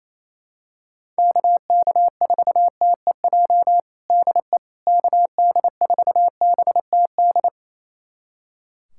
Morse code, heard as a keyed tone, about 28 wpm, 700 Hz. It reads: KK4TEJ DE KD4BTD
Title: USA Callsign DE Callsign - 6 Characters - 28wpm